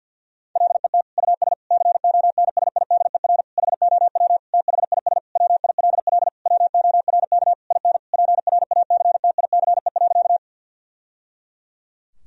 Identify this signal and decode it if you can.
Morse code, heard as a keyed tone, about 50 wpm, 700 Hz. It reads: LET US CONSIDER HOW THIS WILL WORK IN PRACTICE.
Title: Sentences from Top 1000 Words - Review Part 4 - 50wpm